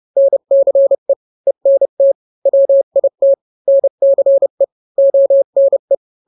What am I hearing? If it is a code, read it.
Morse code, heard as a keyed tone, about 30 wpm, 550 Hz. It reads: NCE ENT WIT NCE ONE
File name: n-grams / 3-grams 30 25 WPM